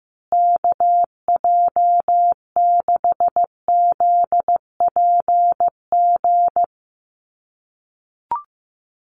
Morse code, heard as a keyed tone, about 15 wpm, 700 Hz. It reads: KJ6ZPG
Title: USA Call Signs Repeated 3x 15wpm